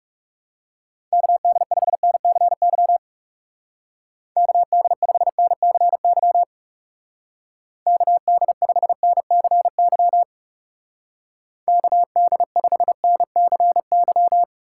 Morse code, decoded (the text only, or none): KD5NCY KD5NCY KD5NCY KD5NCY